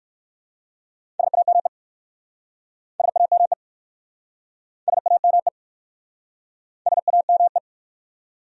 Morse code, heard as a keyed tone, about 59 wpm, 700 Hz. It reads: SAME SAME SAME SAME